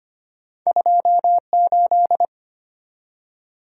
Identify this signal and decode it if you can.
Morse code, heard as a keyed tone, about 25 wpm, 690 Hz. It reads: 28
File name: Two Letter-Number Combinations - 25wpm